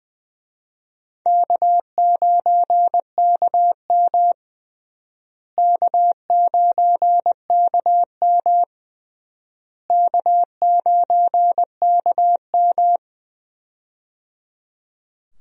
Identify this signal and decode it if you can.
Morse code, heard as a keyed tone, about 21 wpm, 700 Hz. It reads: K9KM K9KM K9KM
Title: USA Call Signs - 4 Characters - Repeated 3x - 20wpm